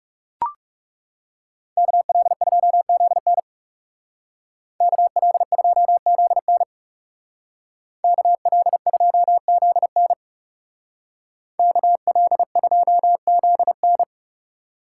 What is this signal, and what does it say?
Morse code, heard as a keyed tone, about 37 wpm, 700 Hz. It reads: KL2ZN KL2ZN KL2ZN KL2ZN
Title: Speed Racing - USA 2x2 Call Signs 30wpm